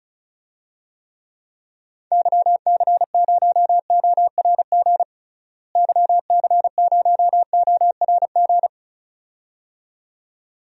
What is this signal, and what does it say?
Morse code, heard as a keyed tone, about 36 wpm, 700 Hz. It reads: YC0ORG YC0ORG
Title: DXCC-100 Call Signs - 6 Characters - Repeated 2x - 35wpm